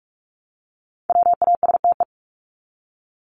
Morse code, heard as a keyed tone, about 44 wpm, 700 Hz. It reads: WASTE
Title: Top 1751-2000 Words 45wpm